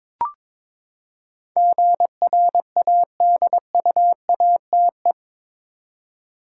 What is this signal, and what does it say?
Morse code, heard as a keyed tone, about 21 wpm, 700 Hz. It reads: GRADUATE
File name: practice / 22wpm / Eight Letter Words - 22wpm